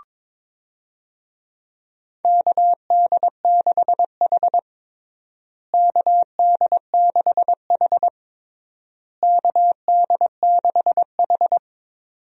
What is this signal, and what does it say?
Morse code, heard as a keyed tone, about 22 wpm, 700 Hz. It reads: KD6H KD6H KD6H
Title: USA Call Signs - 4 Characters - Repeated 3x - 22wpm